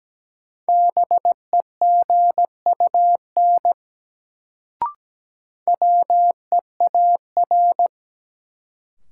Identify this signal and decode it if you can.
Morse code, heard as a keyed tone, about 17 wpm, 700 Hz. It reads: BEGUN WEAR
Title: Sentences from Top 1000 Words - Review Part 6 - New Word Drill - 17wpm